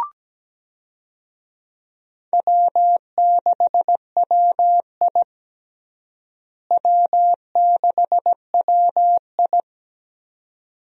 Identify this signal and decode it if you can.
Morse code, heard as a keyed tone, about 17 wpm, 700 Hz. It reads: W6WI W6WI
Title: USA Call Signs - 4 Characters - Repeated 2x - 17wpm